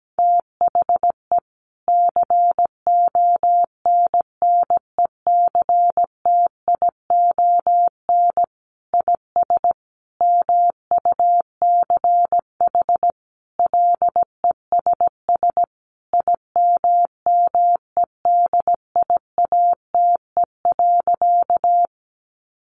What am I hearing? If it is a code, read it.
Morse code, heard as a keyed tone, about 17 wpm, 700 Hz. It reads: THE CONNECTION IS MUCH LESS IMMEDIATE.
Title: Sentences from Top 1300 Words 17wpm